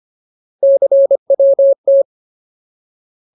Morse code, heard as a keyed tone, about 25 wpm, 550 Hz. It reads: CWT
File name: Ultimate Word List / 25WPM Ultimate